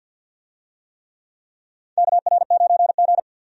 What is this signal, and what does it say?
Morse code, heard as a keyed tone, about 51 wpm, 700 Hz. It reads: KR9G
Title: USA Call Signs - 4 Characters - 50wpm